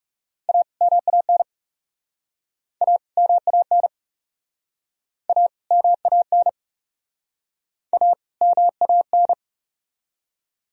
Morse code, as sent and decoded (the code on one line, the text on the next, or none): .- / -- .- -. / .- / -- .- -. / .- / -- .- -. / .- / -- .- -.
A MAN A MAN A MAN A MAN